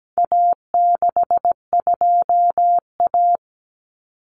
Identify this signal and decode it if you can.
Morse code, heard as a keyed tone, about 17 wpm, 700 Hz. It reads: A62A
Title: Common Cut Numbers - 4-Digit 17wpm